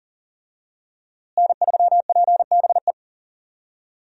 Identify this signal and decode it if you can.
Morse code, heard as a keyed tone, about 38 wpm, 700 Hz. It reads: N3PBE
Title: USA Call Signs - 5 Characters - Repeated 3x - 40wpm